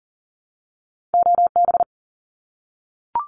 Morse code, heard as a keyed tone, about 40 wpm, 700 Hz. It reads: OB
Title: Abbreviations Repeated 3x - 40wpm